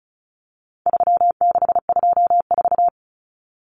Morse code, heard as a keyed tone, about 34 wpm, 700 Hz. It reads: 3624
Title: Common Cut Numbers - 4-Digit 35wpm